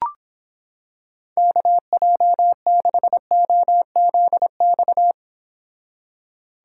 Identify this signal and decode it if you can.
Morse code, heard as a keyed tone, about 26 wpm, 700 Hz. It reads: KJ6OZX